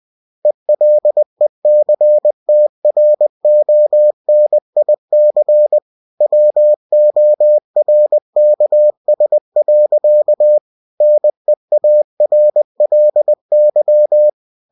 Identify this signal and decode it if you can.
Morse code, heard as a keyed tone, about 20 wpm, 600 Hz. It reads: ELECTRONIC WORKS. NEARLY
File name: cw-books / winnie-the-pooh / winnie-20wpm-page-0114